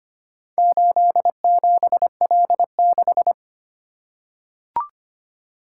Morse code, heard as a keyed tone, about 25 wpm, 700 Hz. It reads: 87L6
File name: practice / 25wpm / Four Letter-Number Combinations - 25wpm